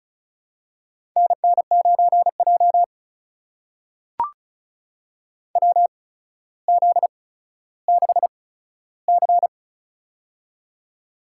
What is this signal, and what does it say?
Morse code, heard as a keyed tone, about 36 wpm, spaced as for 15 wpm, 700 Hz. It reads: NN9J WZ6C
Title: USA Call Signs - 4 Characters - 8x Character Spacing - 35wpm